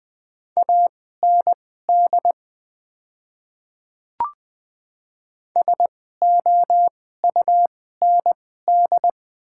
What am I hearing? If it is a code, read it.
Morse code, heard as a keyed tone, about 20 wpm, spaced as for 14 wpm, 700 Hz. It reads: AND SOUND
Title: 2x Character Spacing - Top 200 Words - 20wpm